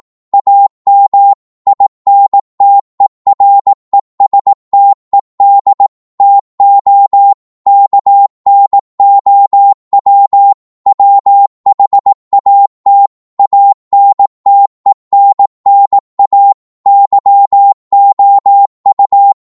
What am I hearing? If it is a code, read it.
Morse code, heard as a keyed tone, about 18 wpm, 800 Hz. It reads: AM INTERESTED TO KNOW WHAT ANTENNA YOU
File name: Morse / qso-18wpm